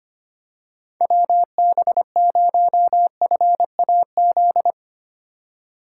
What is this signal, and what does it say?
Morse code, heard as a keyed tone, about 25 wpm, 700 Hz. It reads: WB0FAZ
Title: USA Call Signs - 6 Characters - 2x Character Spacing - 25wpm